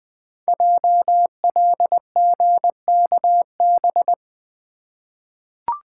Morse code, heard as a keyed tone, about 20 wpm, 700 Hz. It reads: JLGKB
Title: Code Groups - 5 Letters - 20wpm